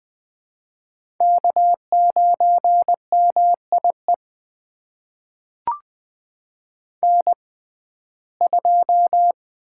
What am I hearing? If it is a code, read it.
Morse code, heard as a keyed tone, about 20 wpm, spaced as for 14 wpm, 700 Hz. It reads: K9MIE N2
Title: USA Call Signs - 5 Characters - 6x Character Spacing - 20wpm